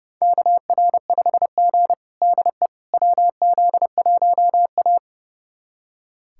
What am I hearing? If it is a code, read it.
Morse code, heard as a keyed tone, about 30 wpm, 700 Hz. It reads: KR5G DE WZ1A
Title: USA Callsign DE Callsign - 4 Characters - 30wpm